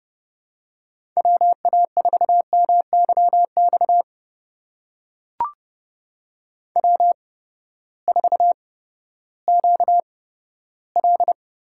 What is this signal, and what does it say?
Morse code, heard as a keyed tone, about 30 wpm, spaced as for 15 wpm, 700 Hz. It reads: WA4MYX W4QL
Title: USA Call Signs - 8x Character Spacing - 30wpm